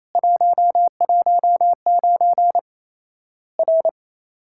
Morse code, heard as a keyed tone, about 28 wpm, 700 Hz. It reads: 119 R
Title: Worked All Europe DX Contest - QTC Exchange - 28wpm